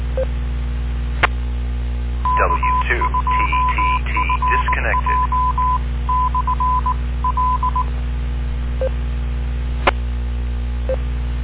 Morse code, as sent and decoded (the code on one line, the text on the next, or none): . / -.- ..--- ..-. .--- -..-. .-.. / .
E K2FJ/L E